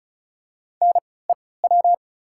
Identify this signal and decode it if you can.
Morse code, heard as a keyed tone, about 36 wpm, spaced as for 20 wpm, 700 Hz. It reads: NEW
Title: 3x Character Spacing - Top 300 Words - 35wpm